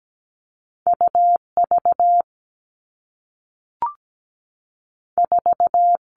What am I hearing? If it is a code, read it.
Morse code, heard as a keyed tone, about 17 wpm, 700 Hz. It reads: UV 4
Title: All Cut Numbers - 2-Digit 17wpm